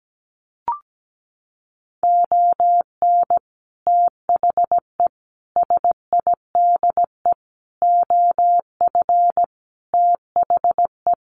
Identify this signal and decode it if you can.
Morse code, heard as a keyed tone, about 17 wpm, 700 Hz. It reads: ON THE SIDE OF THE
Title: Sets of 5 Words 17wpm